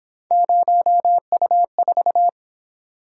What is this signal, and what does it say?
Morse code, heard as a keyed tone, about 27 wpm, 700 Hz. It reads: 0U4
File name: Speed-Racing - Three Letter-Number Combinations - 22wpm